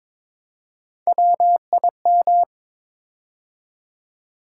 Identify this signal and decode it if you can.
Morse code, heard as a keyed tone, about 22 wpm, 700 Hz. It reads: WIM
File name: practice / 22wpm / Three Letter-Number Combinations - 22wpm